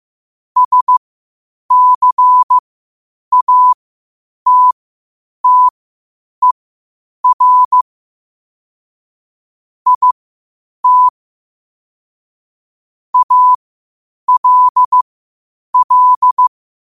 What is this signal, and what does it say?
Morse code, heard as a keyed tone, about 15 wpm, spaced as for 8 wpm, 1000 Hz. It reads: SCATTER IT ALL
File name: qotdfiles / 8wpm